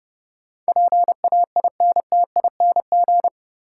Morse code, heard as a keyed tone, about 30 wpm, 700 Hz. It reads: PAINTING
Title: Sentences from Top 1000 Words - Review Part 6 - New Word Drill - 30wpm